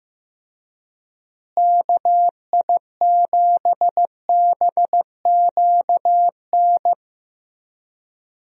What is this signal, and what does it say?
Morse code, heard as a keyed tone, about 15 wpm, 700 Hz. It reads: KI7BQN
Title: USA Call Signs - 6 Characters - Repeated 3x - 15wpm